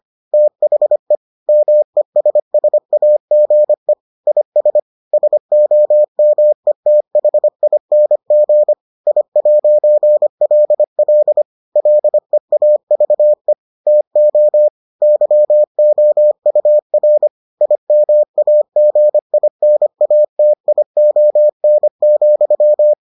Morse code, heard as a keyed tone, about 25 wpm, 600 Hz. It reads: THE MESSAGE IS SOMETHING I'LL LEAVE TO YOUR IMAGINATION,